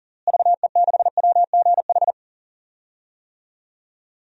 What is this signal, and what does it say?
Morse code, heard as a keyed tone, about 40 wpm, 700 Hz. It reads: VE6WGH
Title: DXCC-100 Call Signs - 6 Characters - Repeated 2x - 40wpm